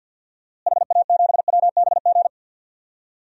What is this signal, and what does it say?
Morse code, heard as a keyed tone, about 51 wpm, 700 Hz. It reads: HA7WBG
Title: DXCC-100 Call Signs - 6 Characters - Repeated 2x - 50wpm